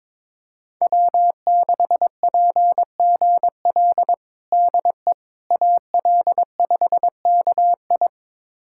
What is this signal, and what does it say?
Morse code, heard as a keyed tone, about 22 wpm, 700 Hz. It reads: W6PGL DE AL5KI
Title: USA Callsign DE Callsign - 5 Characters - 22wpm